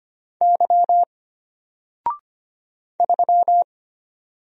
Morse code, heard as a keyed tone, about 25 wpm, 700 Hz. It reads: Y 3
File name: Single-Letter-Number-25wpm